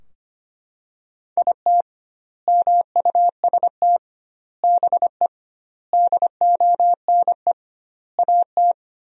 Morse code, heard as a keyed tone, about 25 wpm, 700 Hz. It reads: IT MUST BE DONE AT
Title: 2x Word-Spacing - Sentences from Top 500 Words - 25wpm